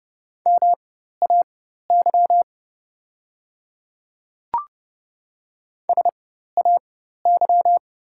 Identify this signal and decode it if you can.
Morse code, heard as a keyed tone, about 30 wpm, spaced as for 14 wpm, 700 Hz. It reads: MAY SAY
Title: 4x Character Spacing - Top 500 Words - 30wpm